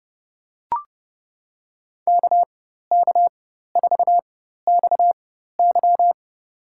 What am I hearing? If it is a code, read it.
Morse code, heard as a keyed tone, about 32 wpm, spaced as for 14 wpm, 700 Hz. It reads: KK4XY
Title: USA Call Signs - 5 Characters - 4x Character Spacing - 30wpm